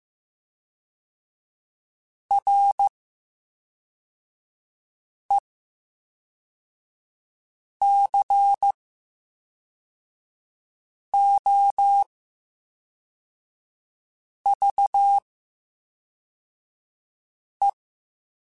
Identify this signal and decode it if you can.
Morse code, heard as a keyed tone, about 15 wpm, spaced as for 3 wpm, 800 Hz. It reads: RECOVE
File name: Special INGs / Special INGs-mp3s / VerySlowToFaster Ing @10